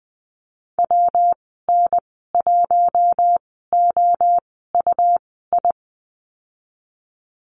Morse code, heard as a keyed tone, about 20 wpm, spaced as for 15 wpm, 700 Hz. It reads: WN1OUI
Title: 128 Lesson 21g - Number 6 - Call Signs 20wpm